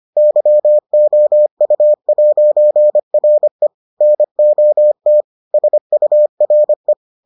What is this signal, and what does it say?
Morse code, heard as a keyed tone, about 25 wpm, 600 Hz. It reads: YOU'RE NOT SURE